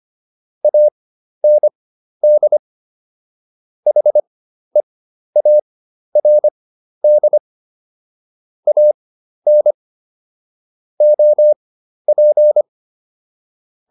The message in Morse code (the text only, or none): AND HEARD AN OP